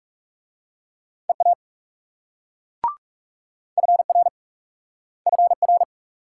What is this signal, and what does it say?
Morse code, heard as a keyed tone, about 38 wpm, 700 Hz. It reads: EA FR FR